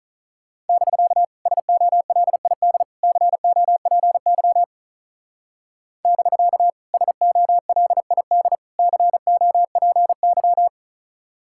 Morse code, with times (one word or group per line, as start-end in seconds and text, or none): 0.7-1.2 <BK>
1.5-2.8 SOLID
3.0-4.6 COPY
6.1-6.7 <BK>
6.9-8.6 SOLID
8.8-10.7 COPY